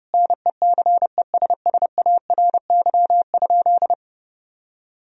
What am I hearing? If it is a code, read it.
Morse code, heard as a keyed tone, about 32 wpm, 700 Hz. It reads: NECESSARY?
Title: Sentences from Top 1000 Words - Review Part 3 - 30wpm